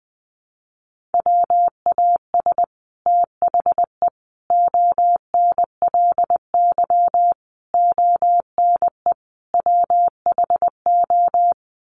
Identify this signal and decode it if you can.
Morse code, heard as a keyed tone, about 20 wpm, 700 Hz. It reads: WAS THE ONLY ONE WHO
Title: Sets of 5 Words Encore2 20wpm